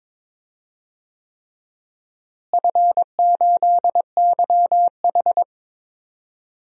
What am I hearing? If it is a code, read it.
Morse code, heard as a keyed tone, about 21 wpm, 700 Hz. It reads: F8YH
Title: DXCC-100 Call Signs - 4 Characters - Repeated 3x - 22wpm